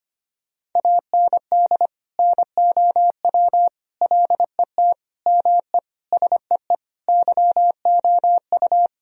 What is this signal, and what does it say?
Morse code, heard as a keyed tone, about 25 wpm, 700 Hz. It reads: AND NOW LET ME SEE YOU
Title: Sentences from Top 1000 Words - Review Part 2 - 25wpm